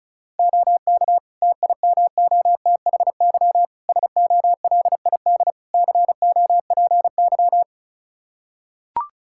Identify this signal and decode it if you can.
Morse code, heard as a keyed tone, about 34 wpm, 700 Hz. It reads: OK TIMOTHY SOLID COPY
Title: K1USN SST QSO Practice 35wpm